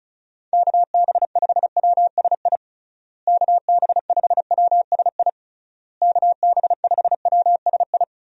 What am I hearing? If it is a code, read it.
Morse code, heard as a keyed tone, about 36 wpm, 700 Hz. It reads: KB5WSI KB5WSI KB5WSI
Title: USA Call Signs - 6 Characters - Repeated 3x - 35wpm